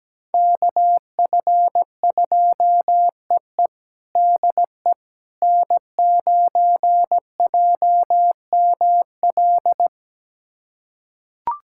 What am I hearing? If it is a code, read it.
Morse code, heard as a keyed tone, about 17 wpm, 700 Hz. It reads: KF2EE DE N9JML
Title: USA Callsign DE Callsign - 5 Characters - 17wpm